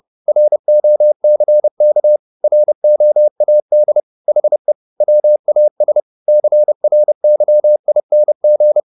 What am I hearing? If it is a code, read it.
Morse code, heard as a keyed tone, about 30 wpm, 600 Hz. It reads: ROCK ROAD HE WAS CRYING